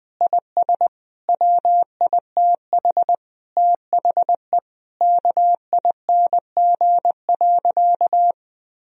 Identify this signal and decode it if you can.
Morse code, heard as a keyed tone, about 20 wpm, 700 Hz. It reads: IS WITH THE KING.
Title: Sentences from Top 1000 Words - Review Part 3 - 20wpm